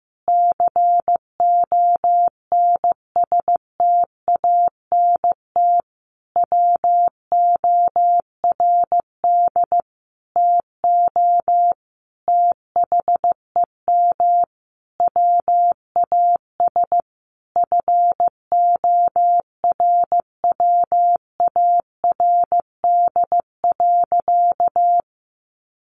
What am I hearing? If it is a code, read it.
Morse code, heard as a keyed tone, about 15 wpm, 700 Hz. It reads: CONSTANT WORD TO THEM WAS FORWARD.
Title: Sentences from Top 1500 Words 15wpm